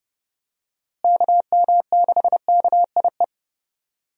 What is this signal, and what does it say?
Morse code, heard as a keyed tone, about 29 wpm, 700 Hz. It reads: KM6KIE
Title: USA Call Signs - 6 Characters - Repeated 3x - 30wpm